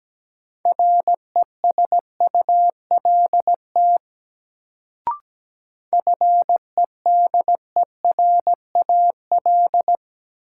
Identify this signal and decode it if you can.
Morse code, heard as a keyed tone, about 17 wpm, 700 Hz. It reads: RESULT FEDERAL
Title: Sets of 3 Words - New Word Drill - 17wpm